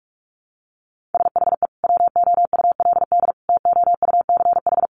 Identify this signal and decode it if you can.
Morse code, heard as a keyed tone, about 44 wpm, 700 Hz. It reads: SHE WOULD TOUCH
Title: Sentences from Top 1200 Words 45wpm